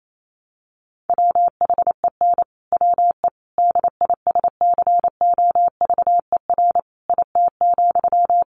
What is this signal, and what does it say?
Morse code, heard as a keyed tone, about 28 wpm, 700 Hz. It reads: WHEN WE DISCOVER IT,
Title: Sentences from Top 1100 Words 28wpm